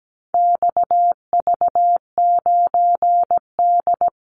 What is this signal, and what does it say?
Morse code, heard as a keyed tone, about 17 wpm, 700 Hz. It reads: XV9D
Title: Call Signs Repeated 3x - 17wpm